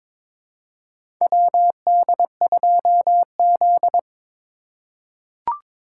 Morse code, heard as a keyed tone, about 23 wpm, 700 Hz. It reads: WD2Z
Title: USA Call Signs - 4 Characters - 3x Character Spacing - 22wpm